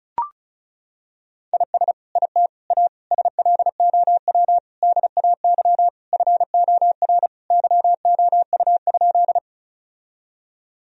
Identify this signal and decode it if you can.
Morse code, heard as a keyed tone, about 36 wpm, 700 Hz. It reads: IS IT A SLOW DAY FOR YOU?
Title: Sentences from Top 1000 Words - Review Part 2 - 35wpm